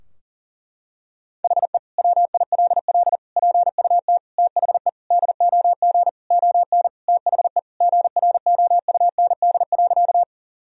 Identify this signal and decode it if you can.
Morse code, heard as a keyed tone, about 38 wpm, 700 Hz. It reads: HE WILL PUT THE DOG ON THE GROUND.
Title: Sentences from Top 1000 Words - Review Part 2 - 40wpm